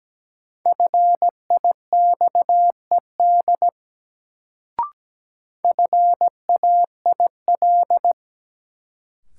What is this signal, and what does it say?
Morse code, heard as a keyed tone, about 17 wpm, 700 Hz. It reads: FIXED FAIL
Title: Sentences from Top 1100 Words - New Word Drill Alt - 17wpm